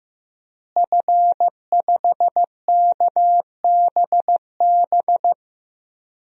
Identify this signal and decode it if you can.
Morse code, heard as a keyed tone, about 15 wpm, 700 Hz. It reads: F5KBB
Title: DXCC-100 Call Signs - 5 Characters - Repeated 3x - 15wpm